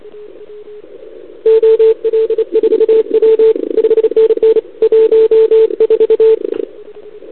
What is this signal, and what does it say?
Morse code, heard as a keyed tone, about 30 wpm, 440 Hz. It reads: OL4W 5NN 14
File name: CQWWCW2021 / Doderna OL3Z